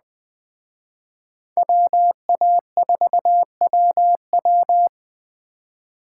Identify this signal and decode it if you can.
Morse code, heard as a keyed tone, about 21 wpm, 700 Hz. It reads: WA4WW
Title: Speed Racing - USA 2x2 Call Signs 17wpm